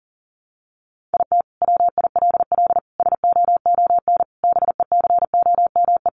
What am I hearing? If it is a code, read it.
Morse code, heard as a keyed tone, about 37 wpm, 700 Hz. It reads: IT WILL SOON BECOME
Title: Sentences from Top 1000 Words 40wpm